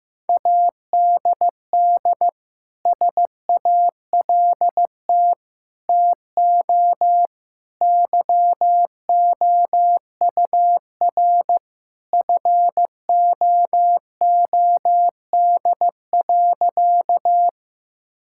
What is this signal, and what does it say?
Morse code, heard as a keyed tone, about 16 wpm, 700 Hz. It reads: ADD SALT TO YOUR FOOD.
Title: Sentences from Top 1000 Words - Review Part 5 - 15wpm